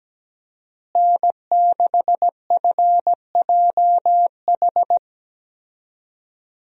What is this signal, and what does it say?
Morse code, heard as a keyed tone, about 17 wpm, 700 Hz. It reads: N6FJH
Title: USA Call Signs - 5 Characters - Repeated 3x - 17wpm